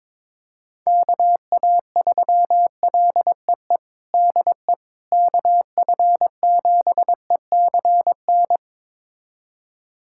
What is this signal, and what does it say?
Morse code, heard as a keyed tone, about 22 wpm, 700 Hz. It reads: KA3LEE DE KF7ECN
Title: USA Callsign DE Callsign - 6 Characters - 22wpm